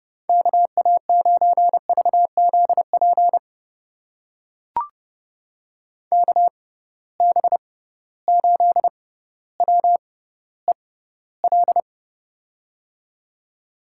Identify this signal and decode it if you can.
Morse code, heard as a keyed tone, about 32 wpm, spaced as for 16 wpm, 700 Hz. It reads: KA9VZP KB8WEL
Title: USA Call Signs - 6 Characters - 6x Character Spacing - 30wpm